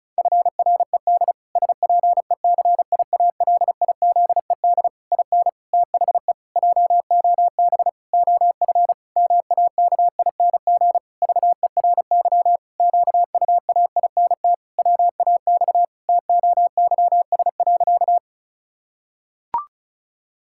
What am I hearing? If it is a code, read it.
Morse code, heard as a keyed tone, about 34 wpm, 700 Hz. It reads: FRED SPECIALIZED IN THE JOB OF MAKING VERY QUAINT WAX TOYS.
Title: Pangram Challenge v2 - 35wpm